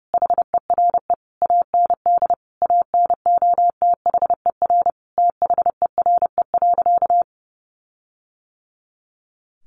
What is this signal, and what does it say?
Morse code, heard as a keyed tone, about 30 wpm, 700 Hz. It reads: HERE AND ANOTHER THERE.
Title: Sentences from Top 1400 Words 30wpm